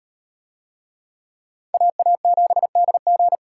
Morse code, heard as a keyed tone, about 37 wpm, 700 Hz. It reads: AA7DG